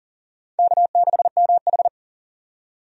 Morse code, heard as a keyed tone, about 37 wpm, 700 Hz. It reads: K6MH
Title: USA Call Signs - 4 Characters - 40wpm